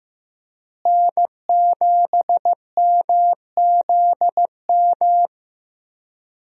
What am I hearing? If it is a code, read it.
Morse code, heard as a keyed tone, about 15 wpm, 700 Hz. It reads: N7MZM
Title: USA Call Signs - 5 Characters - Repeated 3x - 15wpm